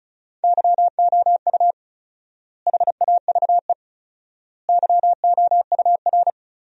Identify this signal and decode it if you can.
Morse code, heard as a keyed tone, about 36 wpm, 700 Hz. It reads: YOU HAVE YOUR
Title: 4x Word-Spacing - Sentences from Top 400 Words - 35wpm